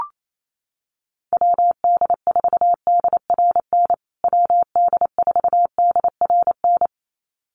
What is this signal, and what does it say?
Morse code, heard as a keyed tone, about 28 wpm, 700 Hz. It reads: WD4DRN WD4DRN
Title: USA Call Signs Repeated 2x 28wpm